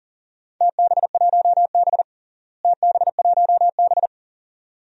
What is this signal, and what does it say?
Morse code, heard as a keyed tone, about 40 wpm, 700 Hz. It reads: TB1B TB1B
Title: DXCC-100 Call Signs - 4 Characters - Repeated 2x - 40wpm